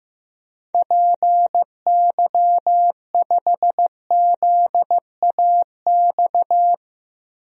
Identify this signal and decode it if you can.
Morse code, heard as a keyed tone, about 15 wpm, 700 Hz. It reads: PY5ZAX
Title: DXCC-100 Call Signs - 6 Characters - 15wpm